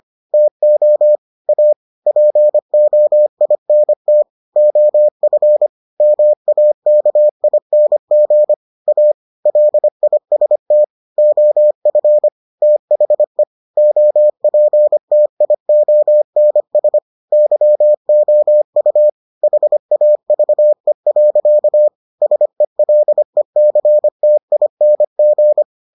Morse code, heard as a keyed tone, about 25 wpm, 600 Hz. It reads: TO A POINT OF MAKING A LIST OF THE OPTIONS YOU HAVE. SELECTING